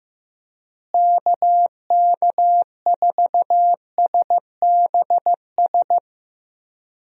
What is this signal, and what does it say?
Morse code, heard as a keyed tone, about 15 wpm, 700 Hz. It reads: KK4SBS
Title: USA Call Signs - 6 Characters - Repeated 3x - 15wpm